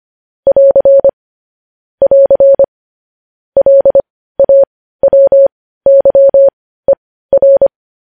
Morse code, heard as a keyed tone, about 25 wpm, spaced as for 15 wpm, 550 Hz. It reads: <AR> <AR> LAWYER
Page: news headlines in morse code at 15 wpm